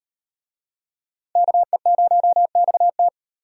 Morse code, heard as a keyed tone, about 38 wpm, 700 Hz. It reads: KE0XT